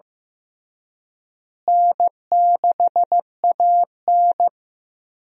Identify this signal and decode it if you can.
Morse code, heard as a keyed tone, about 15 wpm, 700 Hz. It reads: N6AN
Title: USA Call Signs - 4 Characters - Repeated 2x - 15wpm